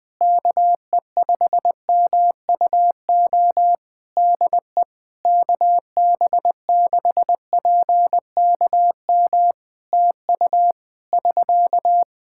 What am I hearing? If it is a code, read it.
Morse code, heard as a keyed tone, about 20 wpm, 700 Hz. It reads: KE5MUO DE KB6PKM TU <SK>
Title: Practice QSO - Exchange 4 - Ending - 20wpm